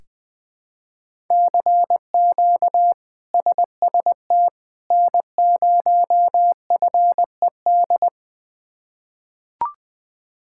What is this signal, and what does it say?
Morse code, heard as a keyed tone, about 20 wpm, 700 Hz. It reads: CQ SST N0FED
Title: K1USN SST QSO Practice - 20wpm